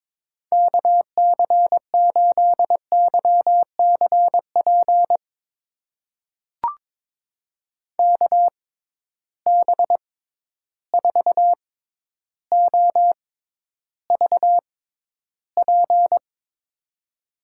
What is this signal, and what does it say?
Morse code, heard as a keyed tone, about 22 wpm, spaced as for 11 wpm, 700 Hz. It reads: KC8YCP KB4OVP
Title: USA Call Signs - 6 Characters - 6x Character Spacing - 22wpm